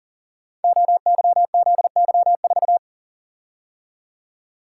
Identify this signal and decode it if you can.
Morse code, heard as a keyed tone, about 42 wpm, 690 Hz. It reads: OYZY4
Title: Code Groups - 5 Letter:Numbers - 40wpm